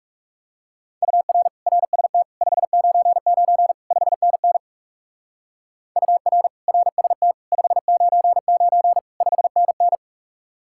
Morse code, heard as a keyed tone, about 38 wpm, 700 Hz. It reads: UR RST 599 5NN UR RST 599 5NN